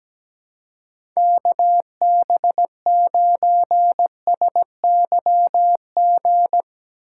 Morse code, heard as a keyed tone, about 17 wpm, 700 Hz. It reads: KB9SYG
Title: USA Call Signs - 6 Characters - 2x Character Spacing - 17wpm